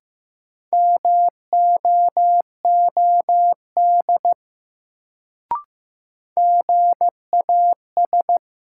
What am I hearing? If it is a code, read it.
Morse code, heard as a keyed tone, about 15 wpm, 700 Hz. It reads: MOOD GAS
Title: Sentences from Top 1100 Words - New Word Drill - 15wpm